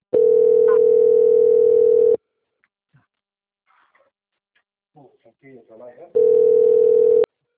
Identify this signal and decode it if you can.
Ringback tone (North America), 440 + 480 Hz, with speech.
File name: out- - - - - .48468